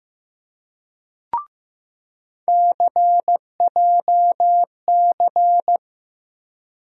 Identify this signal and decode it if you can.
Morse code, heard as a keyed tone, about 15 wpm, 700 Hz. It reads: CJC
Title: USA Call Sign Postfixes - 3 Letters 15wpm